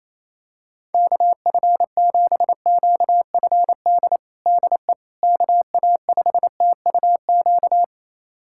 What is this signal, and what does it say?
Morse code, heard as a keyed tone, about 28 wpm, 700 Hz. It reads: KF7QFD DE KA5TUQ
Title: USA Callsign DE Callsign - 6 Characters - 28wpm